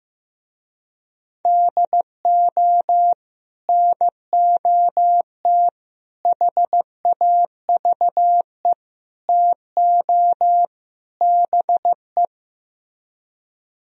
Morse code, heard as a keyed tone, about 15 wpm, 700 Hz. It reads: DO NOT HAVE TO BE
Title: Speed-Racing - Sets of 5 Words 15wpm